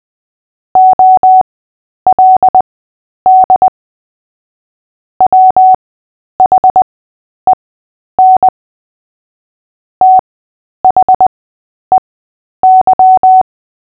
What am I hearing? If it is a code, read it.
Morse code, heard as a keyed tone, about 20 wpm, spaced as for 10 wpm, 750 Hz. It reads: OLD WHEN THEY
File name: Short-Story-5-10wpm